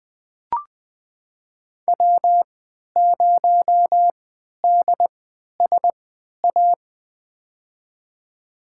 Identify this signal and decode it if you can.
Morse code, heard as a keyed tone, about 20 wpm, spaced as for 11 wpm, 700 Hz. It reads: W0DSA
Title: USA Call Signs - 3x Character Spacing - 20wpm